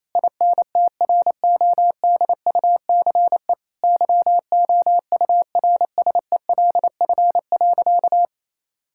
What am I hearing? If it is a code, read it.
Morse code, heard as a keyed tone, about 28 wpm, 700 Hz. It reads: INTRODUCE YOURSELF.
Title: Sentences from Top 1600 Words 28wpm